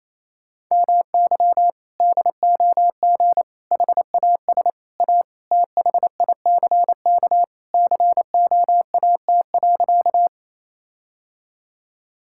Morse code, heard as a keyed tone, about 28 wpm, 700 Hz. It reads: MY DOG HAS A THICK COAT.
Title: Sentences from Top 1600 Words 28wpm